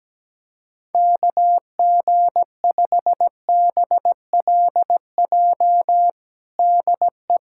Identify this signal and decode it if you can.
Morse code, heard as a keyed tone, about 17 wpm, 700 Hz. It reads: KG5BLJ DE
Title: USA Callsign DE Callsign 17wpm